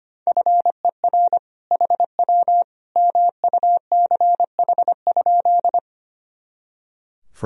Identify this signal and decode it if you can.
Morse code, heard as a keyed tone, about 25 wpm, 700 Hz. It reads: FER HW MUCH?
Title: Sentences from Top 200 Words and Abbreviations 25wpm